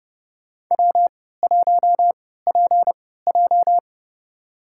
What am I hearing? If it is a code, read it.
Morse code, heard as a keyed tone, about 30 wpm, spaced as for 17 wpm, 700 Hz. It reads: W1PJ
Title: USA Call Signs - 4 Characters - 3x Character Spacing - 30wpm